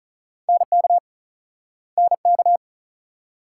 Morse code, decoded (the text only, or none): NK NK